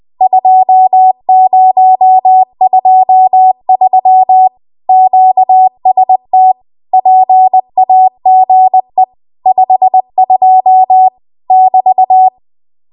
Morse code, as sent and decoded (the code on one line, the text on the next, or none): ..--- ----- ..--- ...-- / --.- ... - / .--. .- --. / ..... ..--- / -...-
2023 QST PAGE 52 <BT>